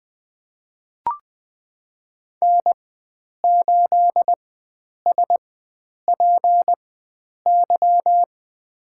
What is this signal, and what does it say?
Morse code, heard as a keyed tone, about 21 wpm, spaced as for 9 wpm, 700 Hz. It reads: N8SPY
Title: USA Call Signs - 5 Characters - 4x Character Spacing - 20wpm